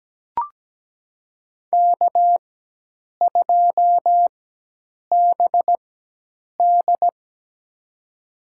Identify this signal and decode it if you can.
Morse code, heard as a keyed tone, about 17 wpm, spaced as for 8 wpm, 700 Hz. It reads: K2BD
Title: USA Call Signs - 4 Characters - 4x Character Spacing - 17wpm